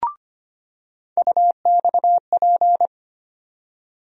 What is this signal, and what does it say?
Morse code, heard as a keyed tone, about 25 wpm, 700 Hz. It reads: UXP
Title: USA Call Sign Postfixes - 3 Letters 25wpm